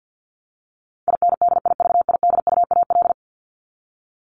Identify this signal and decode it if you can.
Morse code, heard as a keyed tone, about 50 wpm, 700 Hz. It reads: INDIVIDUAL
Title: ES:S Suffix - Top 100 Words with Base Words 50wpm